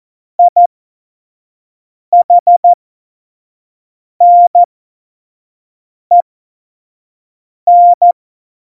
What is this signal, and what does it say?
Morse code, heard as a keyed tone, about 14 wpm, spaced as for 5 wpm, 700 Hz. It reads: IHNEN
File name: Morsekurs / Witze / Witz 06 March 2018, 25 F70